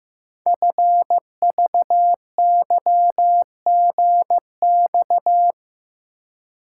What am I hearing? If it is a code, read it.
Morse code, heard as a keyed tone, about 15 wpm, 700 Hz. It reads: FVYGX
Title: Code Groups - 5 Letters - 15wpm